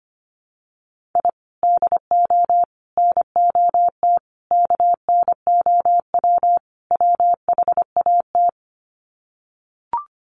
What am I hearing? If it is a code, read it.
Morse code, heard as a keyed tone, about 25 wpm, 700 Hz. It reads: I DO NOT KNOW WHAT
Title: Sets of 5 Words 25wpm